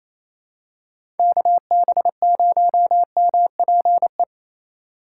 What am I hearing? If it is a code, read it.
Morse code, heard as a keyed tone, about 28 wpm, 700 Hz. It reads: KB0MPE
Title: USA Call Signs - 6 Characters - 8x Character Spacing - 28wpm